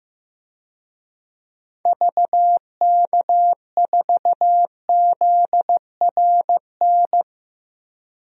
Morse code, heard as a keyed tone, about 15 wpm, 700 Hz. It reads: VK4ZRN
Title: DXCC-100 Call Signs - 6 Characters - Repeated 2x - 15wpm